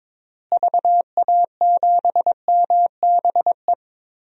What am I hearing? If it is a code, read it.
Morse code, heard as a keyed tone, about 22 wpm, 700 Hz. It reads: VA7MBE
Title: DXCC-100 Call Signs - 6 Characters - Repeated 2x - 22wpm